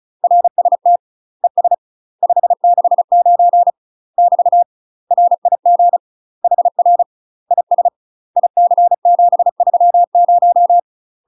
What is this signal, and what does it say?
Morse code, heard as a keyed tone, about 36 wpm, 700 Hz. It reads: RST ES 569 <BT> RIG HR IS IC730
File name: UK MASTER QSOs TRAINING / 35WPM / UK MASTER QSOs 0036